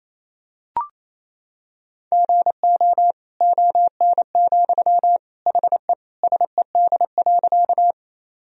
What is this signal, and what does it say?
Morse code, heard as a keyed tone, about 28 wpm, 700 Hz. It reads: GO ON, HE SED.
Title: Sentences from Top 100 Words and Abbreviations 28wpm